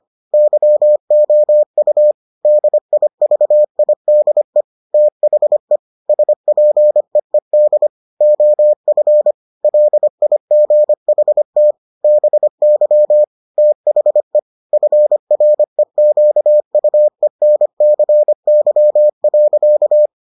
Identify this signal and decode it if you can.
Morse code, heard as a keyed tone, about 25 wpm, 600 Hz. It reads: YOU DIVIDE THE SPEED OF LIGHT BY THE FREQUENCY.